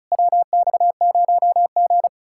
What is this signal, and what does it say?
Morse code, heard as a keyed tone, about 43 wpm, 700 Hz. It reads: WX0G
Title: Speed-Racing Common QSO Phrases - 35wpm